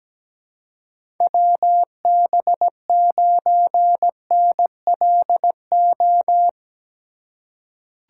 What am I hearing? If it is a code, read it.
Morse code, heard as a keyed tone, about 17 wpm, 700 Hz. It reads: WB9NLO
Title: USA Call Signs - 4x Character Spacing - 17wpm